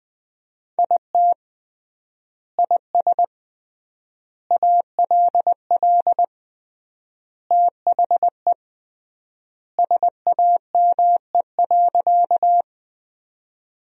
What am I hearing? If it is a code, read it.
Morse code, heard as a keyed tone, about 20 wpm, 700 Hz. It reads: IT IS ALL THE SAME.
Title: 3x Word-Spacing - Sentences from Top 200 Words - 20wpm